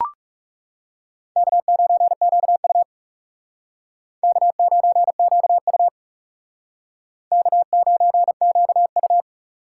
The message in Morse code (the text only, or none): K9QU K9QU K9QU